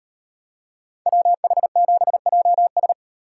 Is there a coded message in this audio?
Yes: Morse code from 1.1 to 2.9 s, heard as a keyed tone, about 37 wpm, 700 Hz. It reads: WH7JS